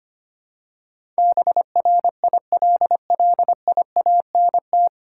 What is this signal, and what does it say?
Morse code, heard as a keyed tone, about 25 wpm, 700 Hz. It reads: BRILLIANT
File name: Nine Letter Words - 25wpm